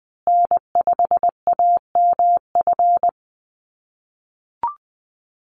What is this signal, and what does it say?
Morse code, heard as a keyed tone, about 20 wpm, 700 Hz. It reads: N5AMF
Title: USA Call Signs Repeated 3x 20wpm